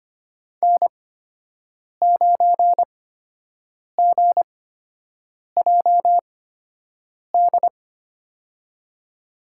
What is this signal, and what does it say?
Morse code, heard as a keyed tone, about 25 wpm, spaced as for 7 wpm, 700 Hz. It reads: N9GJD
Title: USA Call Signs - 5 Characters - 8x Character Spacing - 25wpm